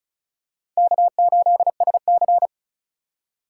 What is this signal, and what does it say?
Morse code, heard as a keyed tone, about 36 wpm, 700 Hz. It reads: K8SC
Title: USA Call Signs - 4 Characters - 3x Character Spacing - 35wpm